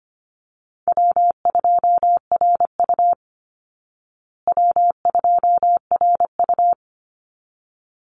Morse code, heard as a keyed tone, about 25 wpm, 700 Hz. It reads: W2RU W2RU
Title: Easy Call Signs Repeated 2x - 25wpm